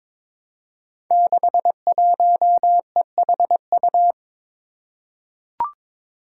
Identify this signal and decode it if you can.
Morse code, heard as a keyed tone, about 22 wpm, 700 Hz. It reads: 61EHU
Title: Five Challenging Letter-Number Combinations 22wpm